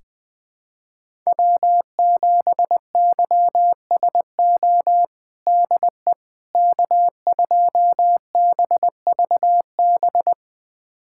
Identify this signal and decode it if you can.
Morse code, heard as a keyed tone, about 20 wpm, 700 Hz. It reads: W7YSO DE K2BVB
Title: USA Callsign DE Callsign - 5 Characters - 20wpm